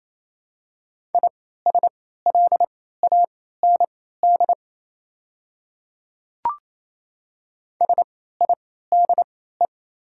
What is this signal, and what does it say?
Morse code, heard as a keyed tone, about 27 wpm, spaced as for 16 wpm, 690 Hz. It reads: ISLAND SIDE
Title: 3x Character Spacing - Top 500 Words - 28wpm